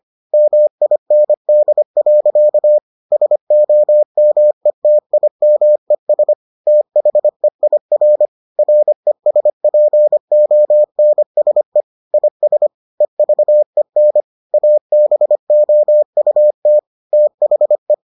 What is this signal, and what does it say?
Morse code, heard as a keyed tone, about 25 wpm, 600 Hz. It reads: MIND. SOMETIMES THEIR RESPONSE IS EVEN ABOUT THE